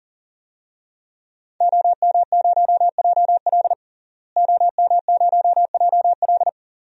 Morse code, heard as a keyed tone, about 40 wpm, 700 Hz. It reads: OM0JL OM0JL
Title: DXCC-100 Call Signs - 5 Characters - Repeated 3x - 40wpm